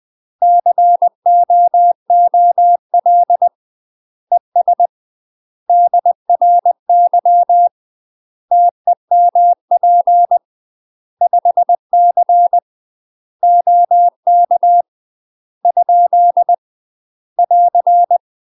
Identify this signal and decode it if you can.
Morse code, heard as a keyed tone, about 20 wpm, 700 Hz. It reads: COOL ES DRY TEMP 5C OK ? <AR>
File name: UK MASTER QSOs TRAINING / 20WPM / UK MASTER QSOs 0073